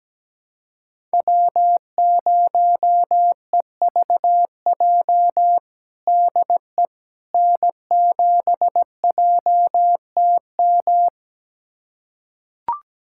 Morse code, heard as a keyed tone, about 17 wpm, 700 Hz. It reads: W0EVJ DE N7JTM
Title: USA Callsign DE Callsign - 5 Characters - 17wpm